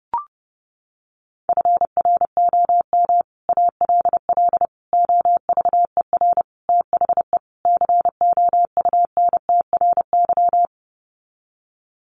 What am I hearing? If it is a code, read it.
Morse code, heard as a keyed tone, about 30 wpm, 700 Hz. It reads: FROM ALL OVER THE COUNTRY
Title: Sets of 5 Words Encore 30wpm